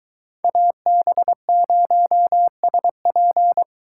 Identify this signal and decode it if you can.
Morse code, heard as a keyed tone, about 22 wpm, 700 Hz. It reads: AB0SP